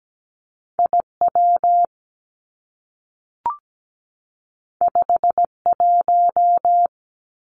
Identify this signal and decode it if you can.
Morse code, heard as a keyed tone, about 17 wpm, 700 Hz. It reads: I W 5 1
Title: Two Challenging Letter-Number Combinations 17wpm